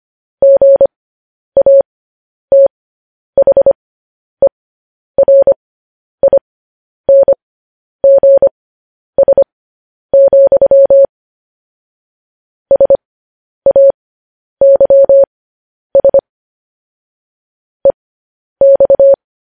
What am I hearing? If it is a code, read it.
Morse code, heard as a keyed tone, about 25 wpm, spaced as for 10 wpm, 550 Hz. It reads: GATHERINGS, SAYS EX
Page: news-headlines-in-morse-code-at-10-wpm